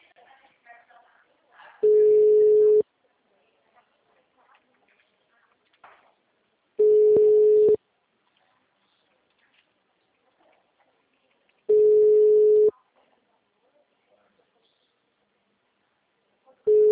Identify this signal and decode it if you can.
Ringback tone (Europe, 425 Hz), with speech.
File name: out- -2018- - - .77